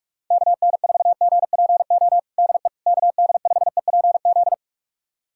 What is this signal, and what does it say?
Morse code, heard as a keyed tone, about 46 wpm, 700 Hz. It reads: KN4GPO DE KD5EPZ
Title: USA Callsign DE Callsign - 6 Characters - 45wpm